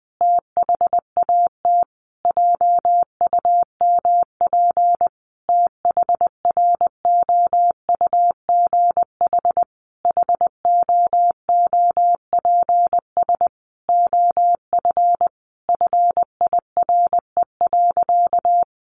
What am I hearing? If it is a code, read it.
Morse code, heard as a keyed tone, about 20 wpm, 700 Hz. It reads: THAT JUMP THROUGH HOOPS OF FIRE.
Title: Easy Phrases 20wpm